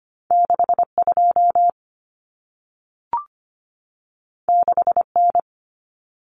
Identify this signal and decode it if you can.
Morse code, heard as a keyed tone, about 25 wpm, 700 Hz. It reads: 62 6N
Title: Common Cut Numbers - 2-Digit 25wpm